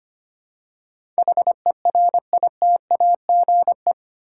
Morse code, heard as a keyed tone, about 25 wpm, 700 Hz. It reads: HERITAGE
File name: Eight Letter Words - 25wpm